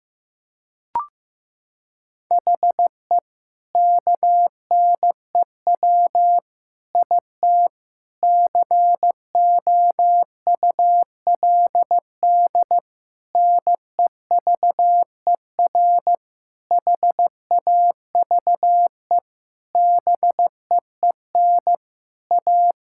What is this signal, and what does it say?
Morse code, heard as a keyed tone, about 16 wpm, 700 Hz. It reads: HE KNEW IT COULD NEVER HAVE BEEN A
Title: Sentences from Top 1000 Words - Review Part 6 - 15wpm